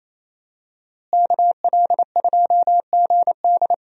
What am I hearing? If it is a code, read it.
Morse code, heard as a keyed tone, about 27 wpm, 700 Hz. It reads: KL2GD
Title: USA Call Signs - 5 Characters - Repeated 2x - 28wpm